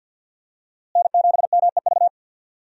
Morse code, heard as a keyed tone, about 52 wpm, 700 Hz. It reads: N7MEV
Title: USA Call Signs - 5 Characters - Repeated 2x - 50wpm